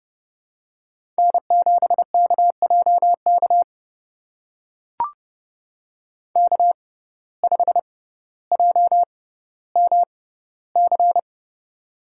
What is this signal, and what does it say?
Morse code, heard as a keyed tone, about 30 wpm, spaced as for 15 wpm, 700 Hz. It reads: N7KJK K5JMC
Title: USA Call Signs - 5 Characters - 6x Character Spacing - 30wpm